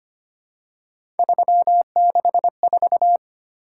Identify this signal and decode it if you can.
Morse code, heard as a keyed tone, about 25 wpm, 700 Hz. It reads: 364
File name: Three Letter-Number Combinations - 25wpm